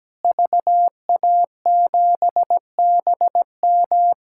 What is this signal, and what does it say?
Morse code, heard as a keyed tone, about 17 wpm, 700 Hz. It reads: VA7BM
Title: K1USN SST QSO Practice - 17wpm